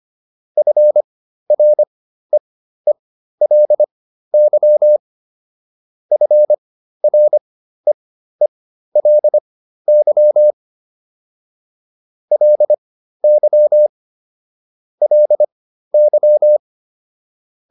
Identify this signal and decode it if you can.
Morse code, heard as a keyed tone, about 25 wpm, spaced as for 13 wpm, 600 Hz. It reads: FREELY FREELY LY LY